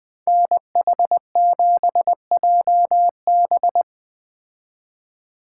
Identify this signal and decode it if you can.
Morse code, heard as a keyed tone, about 19 wpm, 700 Hz. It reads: NH7JB